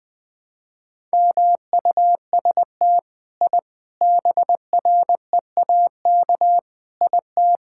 Morse code, heard as a keyed tone, about 19 wpm, 700 Hz. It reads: MUST I BREAK IT
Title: Sentences from Top 1000 Words - Review Part 6 - 20wpm